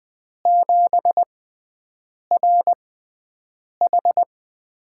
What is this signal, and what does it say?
Morse code, heard as a keyed tone, about 20 wpm, spaced as for 7 wpm, 700 Hz. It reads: 7RH
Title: USA Call Signs - 6x Character Spacing - 20wpm